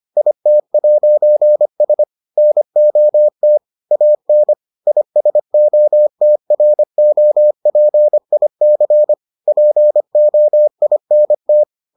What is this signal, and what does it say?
Morse code, heard as a keyed tone, about 25 wpm, 600 Hz. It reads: IT'S NOT AN ISOTROPIC POINT